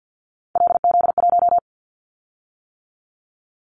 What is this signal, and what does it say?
Morse code, heard as a keyed tone, about 50 wpm, 700 Hz. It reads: LZ1
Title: Callsign Prefixes for Top 100 DXCC - 50wpm